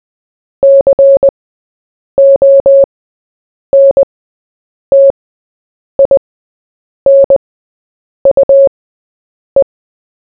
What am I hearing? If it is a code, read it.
Morse code, heard as a keyed tone, about 20 wpm, spaced as for 8 wpm, 550 Hz. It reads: CONTINUE
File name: Short-Story-4-8wpm